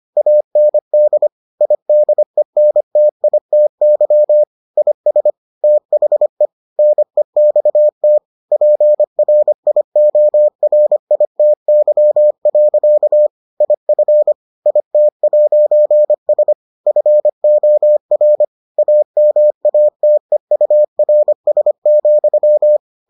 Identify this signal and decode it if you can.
Morse code, heard as a keyed tone, about 25 wpm, 600 Hz. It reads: AND IDENTITY IS THE NEXT PRIORITY. IF IT'S FOR AMATEURS,